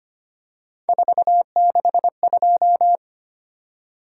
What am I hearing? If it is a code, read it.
Morse code, heard as a keyed tone, about 25 wpm, 700 Hz. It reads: 462
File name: Three Letter-Number Combinations - 25wpm